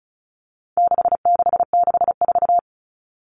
Morse code, heard as a keyed tone, about 34 wpm, 700 Hz. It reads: 6664
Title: Numbers - 4-Digit 35wpm